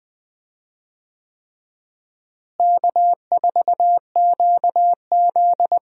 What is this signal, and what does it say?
Morse code, heard as a keyed tone, about 20 wpm, 700 Hz. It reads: K4QZ
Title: USA Call Signs - 4 Characters - Repeated 3x - 20wpm